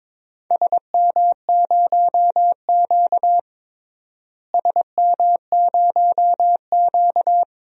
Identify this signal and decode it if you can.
Morse code, heard as a keyed tone, about 22 wpm, 700 Hz. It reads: SM0Q SM0Q
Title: DXCC-100 Call Signs - 4 Characters - Repeated 2x - 22wpm